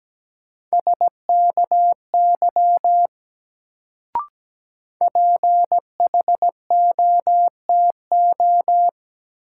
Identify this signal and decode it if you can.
Morse code, heard as a keyed tone, about 17 wpm, 700 Hz. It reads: SKY PHOTO
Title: Sentences from Top 1200 Words - New Word Drill Alt - 17wpm